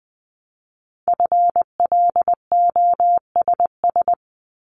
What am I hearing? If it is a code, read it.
Morse code, heard as a keyed tone, about 19 wpm, 700 Hz. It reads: FLOSS
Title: Multi-letter Phonemes Repeated 3x 20wpm